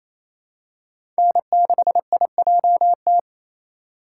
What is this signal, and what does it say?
Morse code, heard as a keyed tone, about 28 wpm, 700 Hz. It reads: N6IJT
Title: USA Call Signs - 5 Characters - 2x Character Spacing - 28wpm